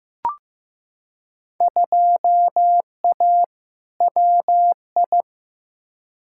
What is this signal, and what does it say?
Morse code, heard as a keyed tone, about 15 wpm, 700 Hz. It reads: 2A WI
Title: Field Day Exchanges Encore - 15wpm